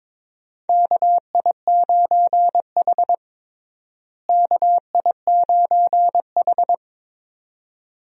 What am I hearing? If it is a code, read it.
Morse code, heard as a keyed tone, about 22 wpm, 700 Hz. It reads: KI9H KI9H
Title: USA Call Signs - 4 Characters - Repeated 2x - 22wpm